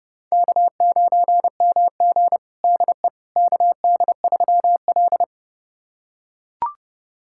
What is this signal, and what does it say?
Morse code, heard as a keyed tone, about 30 wpm, 700 Hz. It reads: K9MG DE KD3L
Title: USA Callsign DE Callsign - 4 Characters - 30wpm